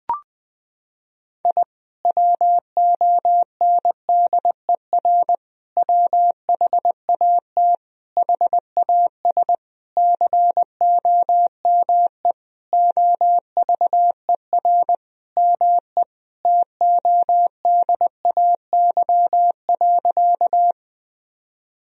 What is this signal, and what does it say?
Morse code, heard as a keyed tone, about 20 wpm, 700 Hz. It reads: I WONDER WHAT HAS COME OVER ME TODAY.
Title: Sentences from Top 1000 Words - Review Part 4 - 20wpm